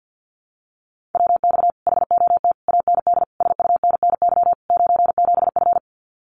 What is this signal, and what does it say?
Morse code, heard as a keyed tone, about 48 wpm, 700 Hz. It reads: WX HOT AND SUNNY 97F
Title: Common QSO Phrases Encore 50wpm